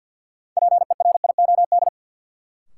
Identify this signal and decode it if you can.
Morse code, heard as a keyed tone, about 48 wpm, 700 Hz. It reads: PERIOD
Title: Sets of 5 Words - New Word Drill - 50wpm